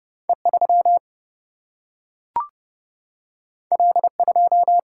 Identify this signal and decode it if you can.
Morse code, heard as a keyed tone, about 30 wpm, 700 Hz. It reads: E 3 L 2
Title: Single-Letter Single-Number 30wpm